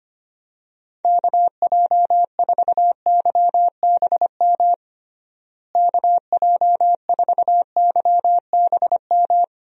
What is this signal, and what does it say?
Morse code, heard as a keyed tone, about 25 wpm, 700 Hz. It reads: KJ4YBM KJ4YBM
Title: USA Call Signs - 6 Characters - Repeated 2x - 25wpm